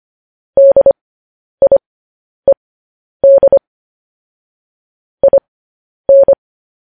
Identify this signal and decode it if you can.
Morse code, heard as a keyed tone, about 25 wpm, spaced as for 10 wpm, 550 Hz. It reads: DIED IN
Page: news headlines in morse code at 10 wpm